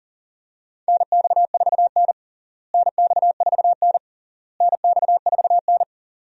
Morse code, heard as a keyed tone, about 40 wpm, 700 Hz. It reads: NX4N NX4N NX4N
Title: USA Call Signs - 4 Characters - Repeated 3x - 40wpm